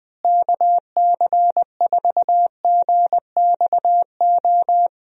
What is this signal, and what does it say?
Morse code, heard as a keyed tone, about 20 wpm, 700 Hz. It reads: KC4GXO
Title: USA Call Signs - 6 Characters - Repeated 3x - 20wpm